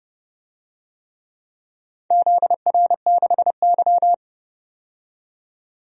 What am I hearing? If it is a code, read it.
Morse code, heard as a keyed tone, about 30 wpm, 700 Hz. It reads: ZR6Y
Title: DXCC-100 Call Signs - 4 Characters - 30wpm